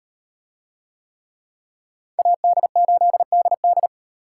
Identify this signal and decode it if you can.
Morse code, heard as a keyed tone, about 38 wpm, 700 Hz. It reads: AD8DD